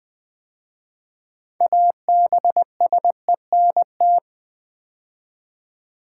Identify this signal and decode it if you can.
Morse code, heard as a keyed tone, about 20 wpm, 700 Hz. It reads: ABSENT
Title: 6 Letter Words - First 3 Letters Given - 20wpm